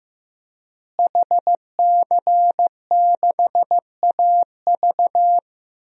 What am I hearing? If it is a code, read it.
Morse code, heard as a keyed tone, about 15 wpm, 700 Hz. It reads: HC6AV
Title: DXCC-100 Call Signs - 5 Characters - Repeated 2x - 15wpm